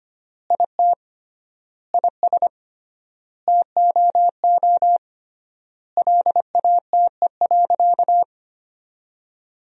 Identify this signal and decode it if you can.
Morse code, heard as a keyed tone, about 25 wpm, 700 Hz. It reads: IT IS TOO LATE.
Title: 3x Word-Spacing - Sentences from Top 300 Words - 25wpm